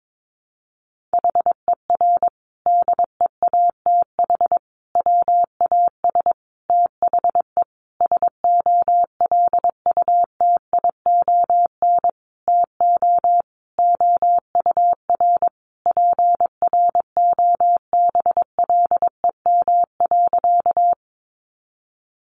Morse code, heard as a keyed tone, about 22 wpm, 700 Hz. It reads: HER DEATH WAS THE SOLUTION TO OUR PROBLEM.
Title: Sentences from Top 1300 Words 22wpm